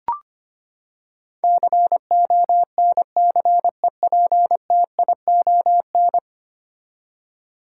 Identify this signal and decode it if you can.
Morse code, heard as a keyed tone, about 25 wpm, 700 Hz. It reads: CONCEPTION
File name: Ten Letter Words - 25wpm